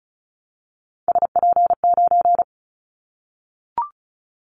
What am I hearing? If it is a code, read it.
Morse code, heard as a keyed tone, about 34 wpm, 700 Hz. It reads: SP9
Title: Callsign Prefixes for Top 25 DXCC - Repeated 3x - 35wpm